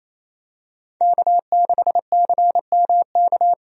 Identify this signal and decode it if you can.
Morse code, heard as a keyed tone, about 28 wpm, 700 Hz. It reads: K6CMK
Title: USA Call Signs - 5 Characters - Repeated 3x - 28wpm